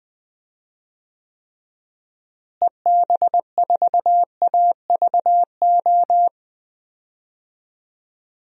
Morse code, heard as a keyed tone, about 20 wpm, 700 Hz. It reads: EB4AVO
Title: DXCC-100 Call Signs - 6 Characters - Repeated 3x - 20wpm